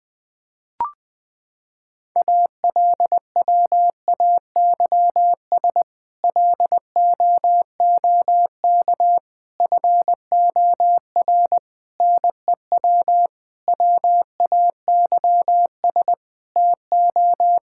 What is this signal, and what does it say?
Morse code, heard as a keyed tone, about 20 wpm, 700 Hz. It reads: ALWAYS LOOK FOR NEW WAYS TO
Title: Sentences with 6-Word Phrases - 20wpm